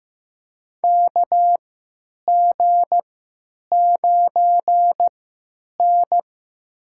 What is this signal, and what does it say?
Morse code, heard as a keyed tone, about 15 wpm, spaced as for 9 wpm, 700 Hz. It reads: KG9N
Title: USA Call Signs - 4 Characters - 3x Character Spacing - 15wpm